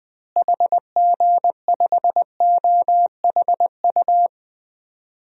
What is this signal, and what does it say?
Morse code, heard as a keyed tone, about 20 wpm, 700 Hz. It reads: HG5OHU
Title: DXCC-100 Call Signs - 6 Characters - Repeated 3x - 20wpm